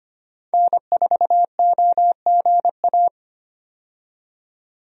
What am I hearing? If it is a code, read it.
Morse code, heard as a keyed tone, about 25 wpm, 700 Hz. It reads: N4OGA
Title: USA Call Signs - 5 Characters - 25wpm